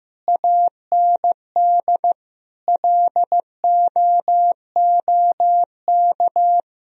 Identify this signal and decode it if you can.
Morse code, heard as a keyed tone, about 16 wpm, 700 Hz. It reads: AND LOOK
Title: Sentences from Top 1000 Words - Review Part 1 - 15wpm